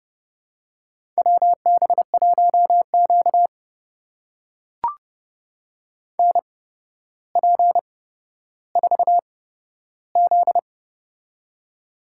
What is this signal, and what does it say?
Morse code, heard as a keyed tone, about 32 wpm, spaced as for 13 wpm, 700 Hz. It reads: WB1Q NP4Z
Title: USA Call Signs - 4 Characters - 8x Character Spacing - 30wpm